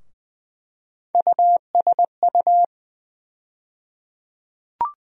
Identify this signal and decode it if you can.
Morse code, heard as a keyed tone, about 20 wpm, 700 Hz. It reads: USU
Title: USA Call Sign Postfixes - 3 Letters 20wpm